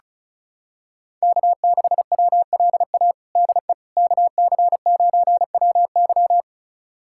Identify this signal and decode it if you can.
Morse code, heard as a keyed tone, about 34 wpm, 690 Hz. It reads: K6WLA DE KC9WY
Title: USA Callsign DE Callsign 35wpm